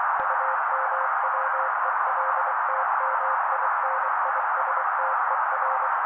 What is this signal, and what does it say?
Morse code, heard as a keyed tone, about 24 wpm, 530 Hz. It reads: UMWELTMINISTER
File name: daily / 23wpm noise